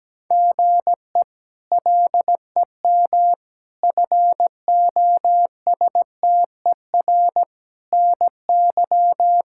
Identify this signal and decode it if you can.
Morse code, heard as a keyed tone, about 17 wpm, 700 Hz. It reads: GE LEM FOSTER NY
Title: K1USN SST QSO Practice - 17wpm